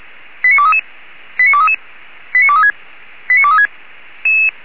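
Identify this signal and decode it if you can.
Five-tone selective call (ZVEI), heard as quick runs of five tones, tones of about 70 ms. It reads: 89233 89233 89236 89236
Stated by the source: Five-tone ZVEI selective call 89241 received on an SDR.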